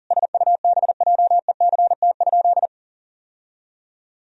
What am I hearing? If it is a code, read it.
Morse code, heard as a keyed tone, about 37 wpm, 700 Hz. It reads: SUBJECT?
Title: Sentences from Top 1000 Words - Review Part 4 - 40wpm